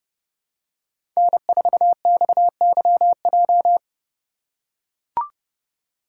Morse code, heard as a keyed tone, about 30 wpm, 700 Hz. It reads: N4XYJ
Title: USA Call Signs - 5 Characters - 4x Character Spacing - 30wpm